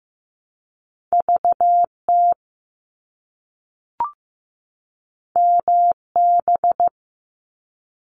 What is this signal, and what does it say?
Morse code, heard as a keyed tone, about 15 wpm, 700 Hz. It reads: VT MB
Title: ARRL Sections 15wpm